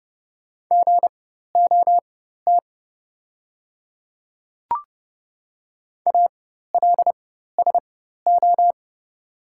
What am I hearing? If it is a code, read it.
Morse code, heard as a keyed tone, about 30 wpm, spaced as for 14 wpm, 700 Hz. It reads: GOT ALSO
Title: 4x Character Spacing - Top 500 Words - 30wpm